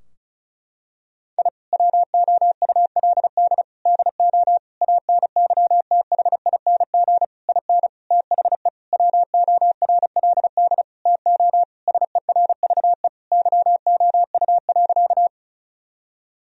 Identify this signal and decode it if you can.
Morse code, heard as a keyed tone, about 34 wpm, 700 Hz. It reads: I WOULD DO ANYTHING IN THE WORLD TO SERVE YOU.
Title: Sentences from Top 1000 Words - Review Part 2 - 35wpm